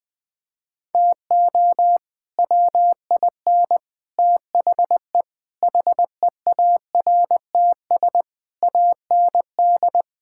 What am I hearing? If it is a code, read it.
Morse code, heard as a keyed tone, about 21 wpm, 700 Hz. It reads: TO WIN THE HEARTS AND
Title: Sets of 8 Words - 20wpm